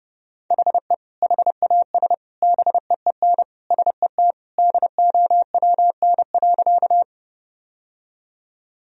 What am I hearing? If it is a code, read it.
Morse code, heard as a keyed tone, about 30 wpm, 700 Hz. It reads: HE HAS BEEN SET DOWN.
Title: Sentences from Top 1000 Words - Review Part 1 - 30wpm